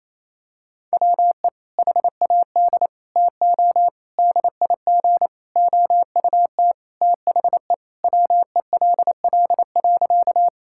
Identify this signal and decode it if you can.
Morse code, heard as a keyed tone, about 28 wpm, 700 Hz. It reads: WE HAD TO DIG OUT THE WELL.
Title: Sentences from Top 1600 Words 28wpm